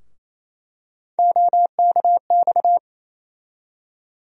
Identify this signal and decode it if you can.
Morse code, heard as a keyed tone, about 28 wpm, 700 Hz. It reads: OKX
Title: USA Call Sign Postfixes - 3 Letters 28wpm